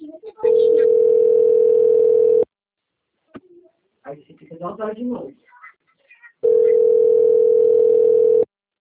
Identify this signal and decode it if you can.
Ringback tone (North America), 440 + 480 Hz, with speech.
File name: rg-1600-unknown- - - .31360